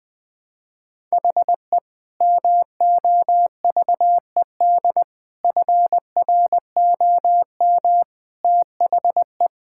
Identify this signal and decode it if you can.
Morse code, heard as a keyed tone, about 21 wpm, 700 Hz. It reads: HE MOVED FROM THE
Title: Sentences with 4-Word Phrases - 20wpm